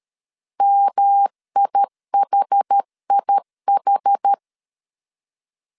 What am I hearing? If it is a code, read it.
Morse code, heard as a keyed tone, about 13 wpm, 800 Hz. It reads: MIHIH
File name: CW / DET / 1 3 1500C S C 12wpm 2750